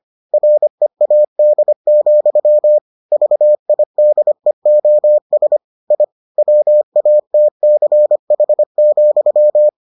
Morse code, heard as a keyed tone, about 25 wpm, 600 Hz. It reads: READ, VIDEOS I WATCH,